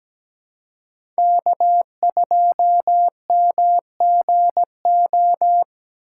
Morse code, heard as a keyed tone, about 17 wpm, 700 Hz. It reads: K2MGO
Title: USA Call Signs - 5 Characters - 3x Character Spacing - 17wpm